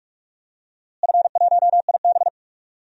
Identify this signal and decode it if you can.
Morse code, heard as a keyed tone, about 41 wpm, 700 Hz. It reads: F1IB
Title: DXCC-100 Call Signs - 4 Characters - Repeated 3x - 45wpm